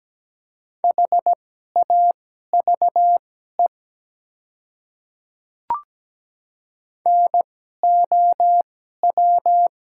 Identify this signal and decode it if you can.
Morse code, heard as a keyed tone, about 17 wpm, spaced as for 12 wpm, 700 Hz. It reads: HAVE NOW
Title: 2x Character Spacing - Top 100 Words - 17wpm